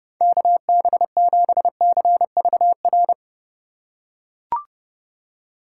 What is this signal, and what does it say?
Morse code, heard as a keyed tone, about 30 wpm, 700 Hz. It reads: KB7CVR
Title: USA Call Signs - 3x Character Spacing - 30wpm